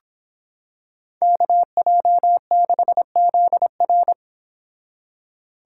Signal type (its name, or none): Morse code